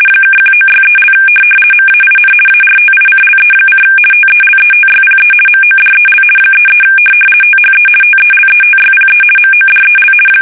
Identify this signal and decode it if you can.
RTTY, about 50 baud. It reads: (1963) MAIN ARTICLE: TELETYPE MODEL 33 THE TELETYPE MODEL
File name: fsk850-50B / teletype-corp